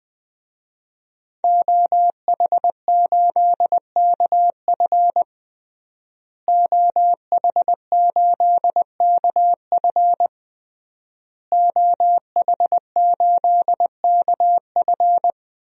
Morse code, heard as a keyed tone, about 21 wpm, 700 Hz. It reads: OH8KF OH8KF OH8KF
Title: DXCC-100 Call Signs - 5 Characters - Repeated 3x - 20wpm